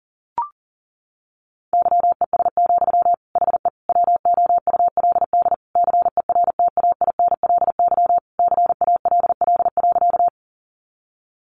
Morse code, heard as a keyed tone, about 40 wpm, 700 Hz. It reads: YES, HE WOULD CERTAINLY CALL.
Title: Sentences from Top 900 Words 40wpm